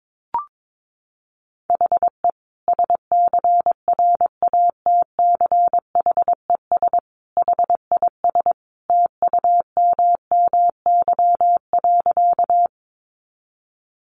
Morse code, heard as a keyed tone, about 22 wpm, 700 Hz. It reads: HE SCRATCHES HIS TUMMY.
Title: Easy Phrases 22wpm